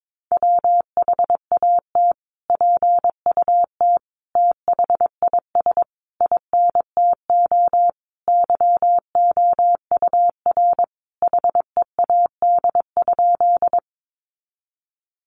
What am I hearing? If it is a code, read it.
Morse code, heard as a keyed tone, about 22 wpm, 700 Hz. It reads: WHAT PUT THIS INTO YOUR HEAD?
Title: Sentences from Top 600 Words 22wpm